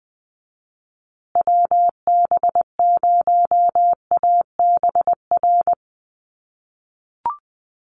Morse code, heard as a keyed tone, about 20 wpm, 700 Hz. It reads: WB0ABR
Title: USA Call Signs Repeated 3x 20wpm